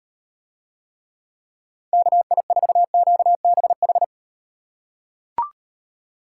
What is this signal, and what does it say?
Morse code, heard as a keyed tone, about 38 wpm, 700 Hz. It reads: KI4QBH